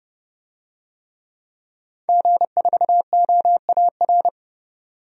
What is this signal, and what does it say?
Morse code, heard as a keyed tone, about 29 wpm, 700 Hz. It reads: G4OAR
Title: DXCC-100 Call Signs - 5 Characters - 30wpm